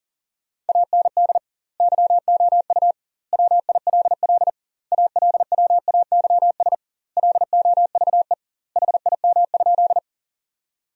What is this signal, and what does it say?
Morse code, heard as a keyed tone, about 40 wpm, 700 Hz. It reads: AND YOU WILL ALWAYS LOVE HIM?
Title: 2x Word-Spacing - Sentences from Top 400 Words - 40wpm